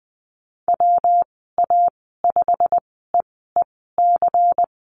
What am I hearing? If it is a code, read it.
Morse code, heard as a keyed tone, about 20 wpm, spaced as for 15 wpm, 700 Hz. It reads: WA5EEC
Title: 128 Lesson 21g - Number 6 - Call Signs 20wpm